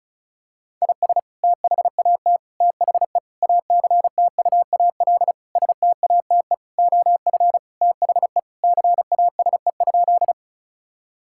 Morse code, heard as a keyed tone, about 36 wpm, 700 Hz. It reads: IS THAT THE ACTUAL STATE OF THE CASE?
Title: Sentences from Top 1000 Words - Review Part 5 - 35wpm